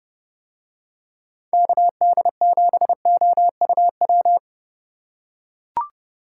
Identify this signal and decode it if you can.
Morse code, heard as a keyed tone, about 28 wpm, 700 Hz. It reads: KD7OUW